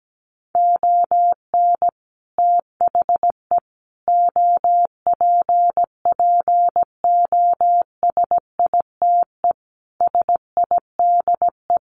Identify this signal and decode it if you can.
Morse code, heard as a keyed tone, about 17 wpm, 700 Hz. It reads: ON THE OPPOSITE SIDE
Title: Sets of 5 Words Encore2 17wpm